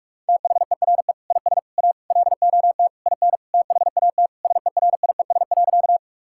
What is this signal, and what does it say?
Morse code, heard as a keyed tone, about 41 wpm, 700 Hz. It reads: THERE IS A LOT IN THAT SERIES.
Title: Sentences from Top 1000 Words - Review Part 5 - 45wpm